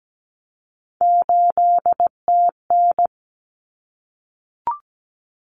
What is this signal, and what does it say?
Morse code, heard as a keyed tone, about 17 wpm, 700 Hz. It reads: 8TN
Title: Common Cut Numbers - 3-Digit 17wpm